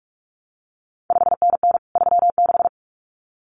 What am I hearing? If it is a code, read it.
Morse code, heard as a keyed tone, about 46 wpm, 700 Hz. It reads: 5NN 36
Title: CQ WW DX Contest Exchange - 45wpm